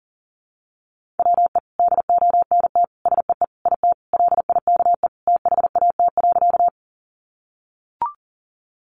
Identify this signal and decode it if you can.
Morse code, heard as a keyed tone, about 40 wpm, 700 Hz. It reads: WE DONT SEE IT LIKE THAT.
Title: Sentences from Top 300 Words 40wpm